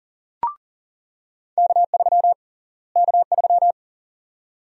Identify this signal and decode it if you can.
Morse code, heard as a keyed tone, about 40 wpm, 700 Hz. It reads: K3 K3
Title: USA Call Sign Prefixes - Repeated 2x - 40wpm